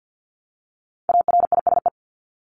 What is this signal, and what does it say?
Morse code, heard as a keyed tone, about 48 wpm, 700 Hz. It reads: ARISE
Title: Top 1501-1750 Words 50wpm